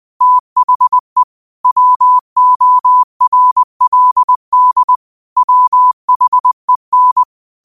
Morse code, heard as a keyed tone, about 20 wpm, 1000 Hz. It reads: THE WORLD WHEN
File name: qotdfiles / 20wpm